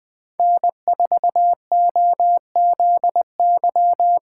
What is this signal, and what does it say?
Morse code, heard as a keyed tone, about 20 wpm, 700 Hz. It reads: N4OZY
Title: USA Call Signs - 5 Characters - 8x Character Spacing - 20wpm